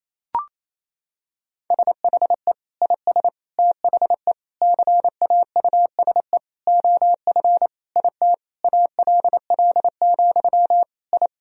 Morse code, heard as a keyed tone, about 28 wpm, 700 Hz. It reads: SHE IS THE CAUSE OF IT ALL, I
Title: Sentences from Top 1000 Words - Review Part 1 - 28wpm